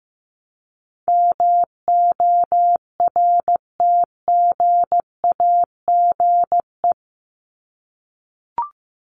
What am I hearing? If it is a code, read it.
Morse code, heard as a keyed tone, about 15 wpm, 700 Hz. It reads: MORTGAGE
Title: Top 1751-2000 Words 15wpm